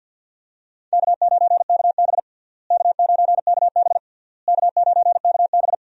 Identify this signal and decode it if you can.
Morse code, heard as a keyed tone, about 53 wpm, 700 Hz. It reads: K9KB K9KB K9KB
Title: USA Call Signs - 4 Characters - Repeated 3x - 50wpm